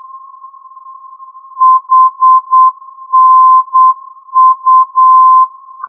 Morse code, heard as a keyed tone, about 8 wpm, 1040 Hz. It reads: HNU
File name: ndb hanau hnu-432